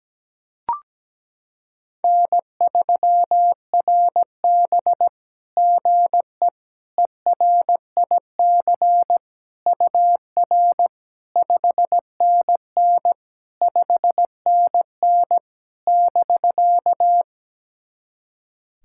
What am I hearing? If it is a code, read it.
Morse code, heard as a keyed tone, about 17 wpm, 700 Hz. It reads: N3RB GE ERIC UR 5NN 5NN <BK>
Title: SOTA QSO Practice 17wpm